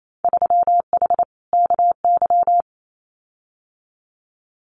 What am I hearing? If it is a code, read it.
Morse code, heard as a keyed tone, about 28 wpm, 700 Hz. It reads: 3H KY
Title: Winter Field Day Exchanges Encore 28wpm